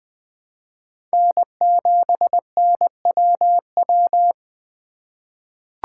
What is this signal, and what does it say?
Morse code, heard as a keyed tone, about 20 wpm, 700 Hz. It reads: N7NWW
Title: USA Call Signs - 5 Characters - 2x Character Spacing - 20wpm